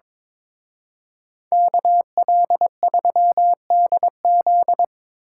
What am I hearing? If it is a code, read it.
Morse code, heard as a keyed tone, about 25 wpm, 700 Hz. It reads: KL3DZ
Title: Speed Racing - USA 2x2 Call Signs 22wpm